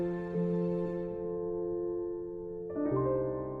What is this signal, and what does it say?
Dial tone (North America), 350 + 440 Hz.